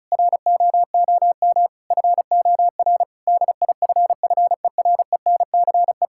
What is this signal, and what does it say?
Morse code, heard as a keyed tone, about 34 wpm, 700 Hz. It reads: ROOM FOR DIFFERENCE
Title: Sentences from Top 1600 Words 35wpm